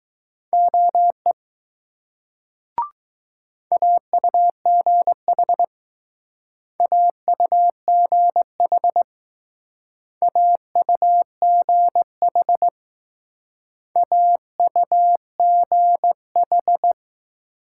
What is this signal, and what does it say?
Morse code, heard as a keyed tone, about 20 wpm, 700 Hz. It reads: OE AUGH AUGH AUGH AUGH